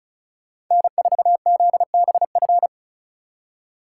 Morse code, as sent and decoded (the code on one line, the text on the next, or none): -. ....- --.. -... ..-.
N4ZBF